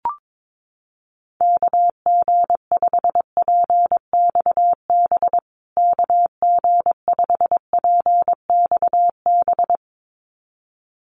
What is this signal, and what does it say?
Morse code, heard as a keyed tone, about 22 wpm, 700 Hz. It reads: KG5PXB KG5PXB
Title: USA Call Signs Repeated 2x 22wpm